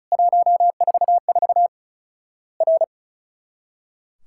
Morse code, heard as a keyed tone, about 36 wpm, 700 Hz. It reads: 144 R
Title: Worked All Europe DX Contest - QTC Exchange - 35wpm